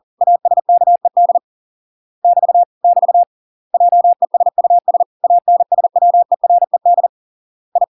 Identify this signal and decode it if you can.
Morse code, heard as a keyed tone, about 40 wpm, 700 Hz. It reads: ASKED <BT> <BT> JESUS ANSWERED I